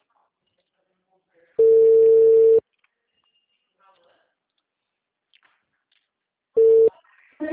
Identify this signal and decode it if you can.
Ringback tone (China, 450 Hz), with speech.